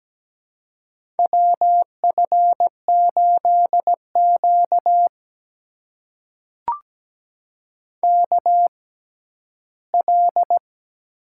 Morse code, heard as a keyed tone, about 17 wpm, spaced as for 12 wpm, 700 Hz. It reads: WF8Q KL
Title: USA Call Signs - 4 Characters - 6x Character Spacing - 17wpm